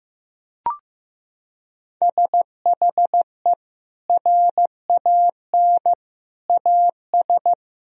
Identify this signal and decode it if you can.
Morse code, heard as a keyed tone, about 16 wpm, 700 Hz. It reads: SHE RAN AS
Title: Sentences from Top 1000 Words - Review Part 4 - 15wpm